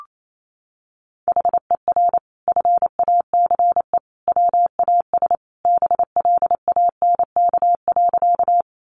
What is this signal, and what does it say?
Morse code, heard as a keyed tone, about 28 wpm, 700 Hz. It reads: HER FACE WAS BLANK.
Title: Sentences from Top 1500 Words 28wpm